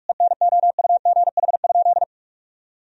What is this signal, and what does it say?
Morse code, heard as a keyed tone, about 46 wpm, 700 Hz. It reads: ENOUGH?
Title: Sentences from Top 1000 Words - Review Part 3 - 45wpm